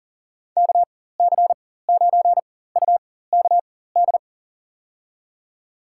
Morse code, heard as a keyed tone, about 38 wpm, spaced as for 18 wpm, 700 Hz. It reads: KC9UKD
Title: USA Call Signs - 6 Characters - 4x Character Spacing - 40wpm